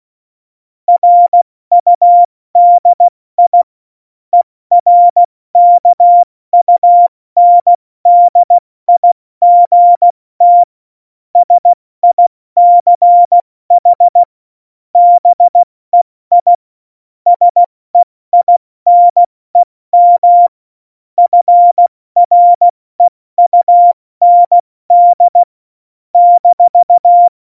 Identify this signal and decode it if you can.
Morse code, heard as a keyed tone, about 16 wpm, spaced as for 14 wpm, 700 Hz. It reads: RUDI ERKUNDIGT SICH BEI SEINEM FREUND -
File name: Morsekurs / Witze / Witz 16. Juli 2019, 70 F80